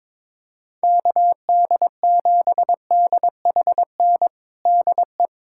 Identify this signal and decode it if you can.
Morse code, heard as a keyed tone, about 22 wpm, 700 Hz. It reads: KD7DHN DE
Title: USA Callsign DE Callsign 22wpm